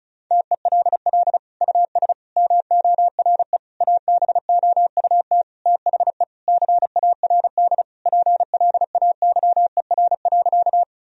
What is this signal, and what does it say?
Morse code, heard as a keyed tone, about 34 wpm, 700 Hz. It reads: TELL US MORE ABOUT THE CARD PLAYER.
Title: Sentences from Top 1000 Words - Review Part 4 - 35wpm